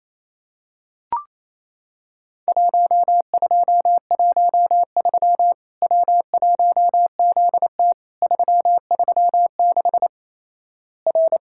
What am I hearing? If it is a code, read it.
Morse code, heard as a keyed tone, about 28 wpm, 700 Hz. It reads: 1213 W1ZT 336 R
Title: Worked All Europe DX Contest - QTC Exchange - 28wpm